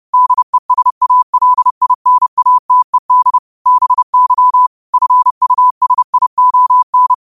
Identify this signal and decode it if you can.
Morse code, heard as a keyed tone, about 30 wpm, 1000 Hz. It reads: DESALINATED BY FUSION
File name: qotdfiles / 30wpm